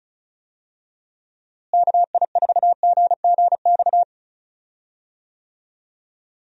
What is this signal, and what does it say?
Morse code, heard as a keyed tone, about 34 wpm, 700 Hz. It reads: KI4GGX
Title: USA Call Signs - 6 Characters - Repeated 3x - 35wpm